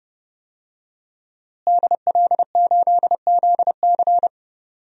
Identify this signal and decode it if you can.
Morse code, heard as a keyed tone, about 30 wpm, 700 Hz. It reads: DL8ZC
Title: DXCC-100 Call Signs - 5 Characters - Repeated 3x - 30wpm